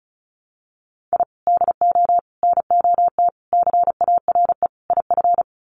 Morse code, heard as a keyed tone, about 34 wpm, 700 Hz. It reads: I DO NOT CARE IF
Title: Sets of 5 Words 35wpm